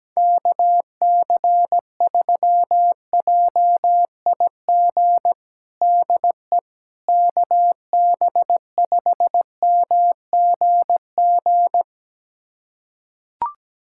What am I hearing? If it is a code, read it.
Morse code, heard as a keyed tone, about 17 wpm, 700 Hz. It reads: KC3JIG DE KB5MGG
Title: USA Callsign DE Callsign 17wpm